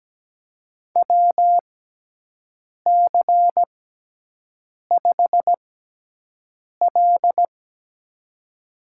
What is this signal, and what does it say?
Morse code, heard as a keyed tone, about 17 wpm, spaced as for 6 wpm, 700 Hz. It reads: WC5L
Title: USA Call Signs - 4 Characters - 6x Character Spacing - 17wpm